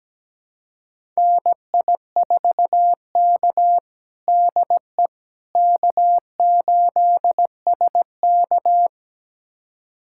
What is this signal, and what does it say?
Morse code, heard as a keyed tone, about 17 wpm, 700 Hz. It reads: NI4K DE K8SK
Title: USA Callsign DE Callsign - 4 Characters - 17wpm